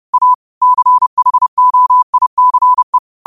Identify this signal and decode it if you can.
Morse code, heard as a keyed tone, about 30 wpm, 1000 Hz. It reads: A CHOICE
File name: qotdfiles / 30wpm